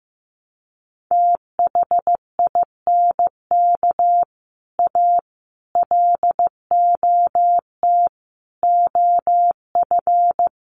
Morse code, heard as a keyed tone, about 15 wpm, 700 Hz. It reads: THINK A LOT OF
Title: Sets of 5 Words 15wpm